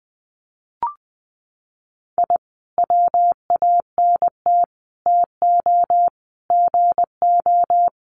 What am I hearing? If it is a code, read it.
Morse code, heard as a keyed tone, about 20 wpm, 700 Hz. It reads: I WANT TO GO
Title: Sets of 5 Words Encore 20wpm